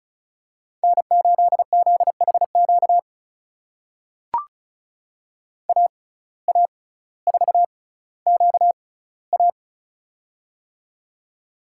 Morse code, heard as a keyed tone, about 36 wpm, spaced as for 18 wpm, 700 Hz. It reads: N8ZHQ AA4QA
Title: USA Call Signs - 5 Characters - 6x Character Spacing - 35wpm